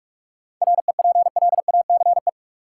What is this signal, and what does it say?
Morse code, heard as a keyed tone, about 46 wpm, 700 Hz. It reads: REPLACE
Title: Sentences from Top 1400 Words - New Word Drill - 45wpm